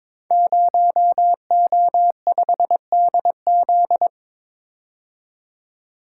Morse code, heard as a keyed tone, about 22 wpm, 700 Hz. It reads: 0O5DZ
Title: Code Groups - 5 Letter:Numbers - 22wpm